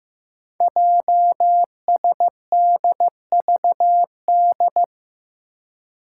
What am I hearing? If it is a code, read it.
Morse code, heard as a keyed tone, about 15 wpm, 700 Hz. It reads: JSDVD
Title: Code Groups - 5 Letters - 15wpm